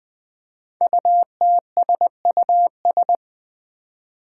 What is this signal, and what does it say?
Morse code, heard as a keyed tone, about 20 wpm, 700 Hz. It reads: UTSUS
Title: Five Challenging Letter-Number Combinations 20wpm